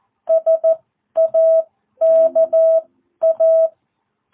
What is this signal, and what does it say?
Morse code, heard as a keyed tone, about 14 wpm, spaced as for 11 wpm, 660 Hz. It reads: SAKA